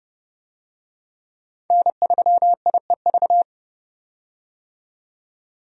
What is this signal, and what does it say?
Morse code, heard as a keyed tone, about 30 wpm, 700 Hz. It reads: N3IEV
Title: USA Call Signs - 5 Characters - 30wpm